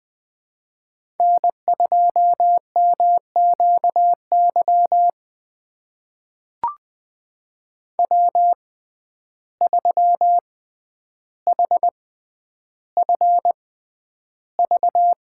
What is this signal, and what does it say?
Morse code, heard as a keyed tone, about 21 wpm, spaced as for 10 wpm, 700 Hz. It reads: N2MQY W3HFV
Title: USA Call Signs - 5 Characters - 6x Character Spacing - 20wpm